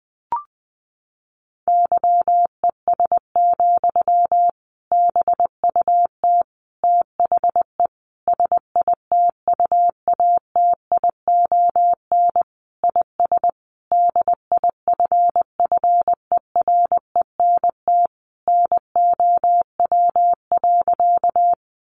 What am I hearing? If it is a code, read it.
Morse code, heard as a keyed tone, about 20 wpm, 700 Hz. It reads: YES, BUT THE SITUATION IS DIFFERENT NOW.
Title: Sentences from Top 1200 Words 20wpm